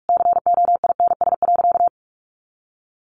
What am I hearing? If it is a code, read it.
Morse code, heard as a keyed tone, about 44 wpm, 700 Hz. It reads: COINS.
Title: A Sound Of Thunder 45wpm